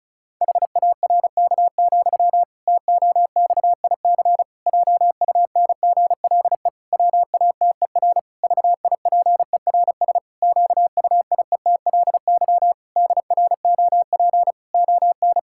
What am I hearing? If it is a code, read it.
Morse code, heard as a keyed tone, about 34 wpm, 700 Hz. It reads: HARK, TOXIC JUNGLE WATER VIPERS QUIETLY DROP ON
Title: Pangram Challenge v2 - 35wpm